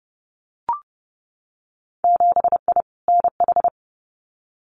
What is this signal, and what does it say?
Morse code, heard as a keyed tone, about 30 wpm, 700 Hz. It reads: 7I NH
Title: Winter Field Day Exchanges Encore 30wpm